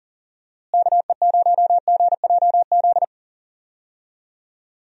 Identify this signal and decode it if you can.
Morse code, heard as a keyed tone, about 40 wpm, 700 Hz. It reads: KE0GJZ
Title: USA Call Signs - 6 Characters - Repeated 2x - 40wpm